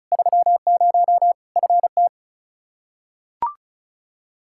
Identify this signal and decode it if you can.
Morse code, heard as a keyed tone, about 36 wpm, 700 Hz. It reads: 30 FT
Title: Practice QSO - Exchange 2 - Info - 35wpm